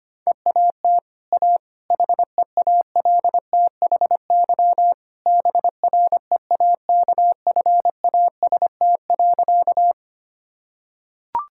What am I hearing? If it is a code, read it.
Morse code, heard as a keyed tone, about 25 wpm, 700 Hz. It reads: EAT A HEALTHY BREAKFAST.
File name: Easy Sentences with Hints - 25wpm